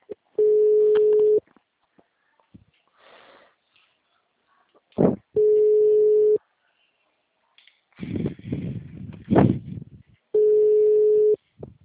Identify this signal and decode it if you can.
Ringback tone (Europe, 425 Hz).
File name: out- -2003- - - .31390